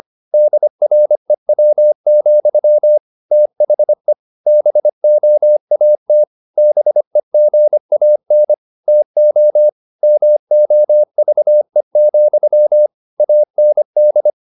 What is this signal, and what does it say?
Morse code, heard as a keyed tone, about 25 wpm, 600 Hz. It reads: DREW, THE BOAT BEGAN TO MOVE, AND